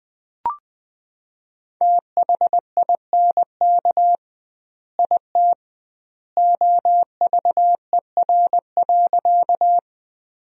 Morse code, heard as a keyed tone, about 20 wpm, 700 Hz. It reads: THINK IT OVER.
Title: 2x Word-Spacing - Sentences from Top 200 Words - 20wpm